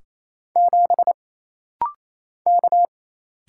Morse code, heard as a keyed tone, about 28 wpm, 700 Hz. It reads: 7 K
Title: Single Letter-Number - Rapid-Fire 28wpm